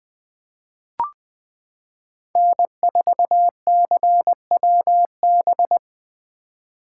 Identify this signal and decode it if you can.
Morse code, heard as a keyed tone, about 20 wpm, 700 Hz. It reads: N4CWB
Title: 2023 POTA Activator Call Signs - 20wpm